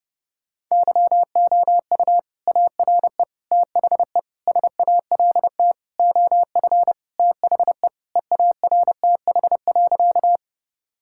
Morse code, heard as a keyed tone, about 30 wpm, 700 Hz. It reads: YOU ARE THE SALT OF THE EARTH.
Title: Sentences from Top 1000 Words - Review Part 5 - 30wpm